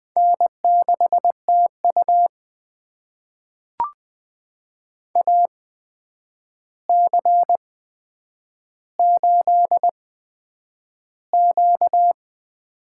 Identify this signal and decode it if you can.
Morse code, heard as a keyed tone, about 20 wpm, spaced as for 9 wpm, 700 Hz. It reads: N6TU AC8Q
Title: USA Call Signs - 4 Characters - 8x Character Spacing - 20wpm